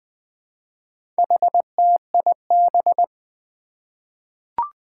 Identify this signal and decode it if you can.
Morse code, heard as a keyed tone, about 20 wpm, 700 Hz. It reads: HTIB
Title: Four Challenging Letter-Number Combinations 20wpm